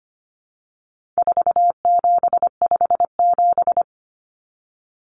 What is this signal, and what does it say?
Morse code, heard as a keyed tone, about 25 wpm, 700 Hz. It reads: 4757
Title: Numbers - 4-Digit 25wpm